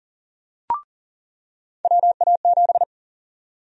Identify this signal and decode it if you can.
Morse code, heard as a keyed tone, about 40 wpm, 700 Hz. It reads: WA7
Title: USA Call Sign Prefixes - 40wpm